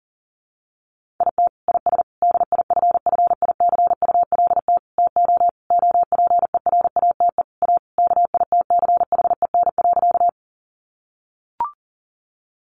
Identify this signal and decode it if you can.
Morse code, heard as a keyed tone, about 40 wpm, 700 Hz. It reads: IT IS DIFFICULT TO OPERATE A KITCHEN.
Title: Sentences from Top 1500 Words 40wpm